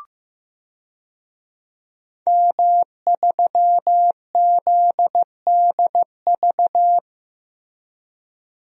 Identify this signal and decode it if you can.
Morse code, heard as a keyed tone, about 15 wpm, 700 Hz. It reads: M3ZDV
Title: DXCC-100 Call Signs - 5 Characters - Repeated 3x - 15wpm